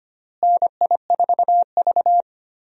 Morse code, heard as a keyed tone, about 25 wpm, 700 Hz. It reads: NI4V
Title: USA Call Signs - 4 Characters - Repeated 2x - 25wpm